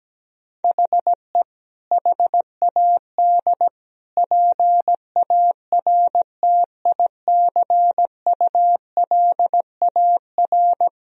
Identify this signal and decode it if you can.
Morse code, heard as a keyed tone, about 17 wpm, 700 Hz. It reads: HE HAD PARTICULAR
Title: Sentences from Top 1000 Words - Review Part 4 - 17wpm